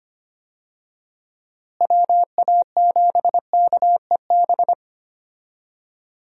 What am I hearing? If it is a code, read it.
Morse code, heard as a keyed tone, about 25 wpm, 700 Hz. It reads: WA7KEB
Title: USA Call Signs - 6 Characters - Repeated 2x - 25wpm